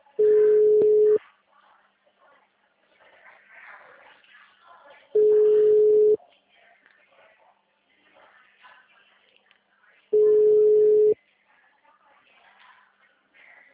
Ringback tone (Europe, 425 Hz).